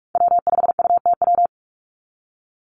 Morse code, heard as a keyed tone, about 46 wpm, 700 Hz. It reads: W5UTW
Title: Call Signs - 45wpm